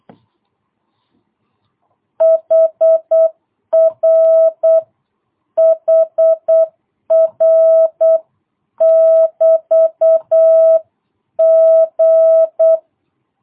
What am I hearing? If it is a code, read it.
Morse code, heard as a keyed tone, about 8 wpm, 650 Hz. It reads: HR HR <BT> G
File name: jh8cbh / 2022cw